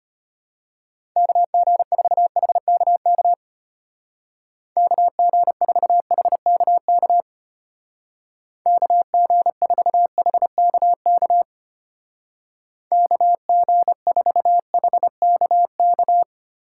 Morse code, heard as a keyed tone, about 30 wpm, 700 Hz. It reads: KG4HKK KG4HKK KG4HKK KG4HKK
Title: Speed Racing - USA 2x3 Call Signs 25wpm